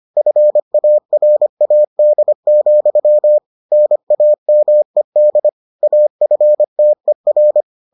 Morse code, heard as a keyed tone, about 25 wpm, 600 Hz. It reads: FARAD, NAMED AFTER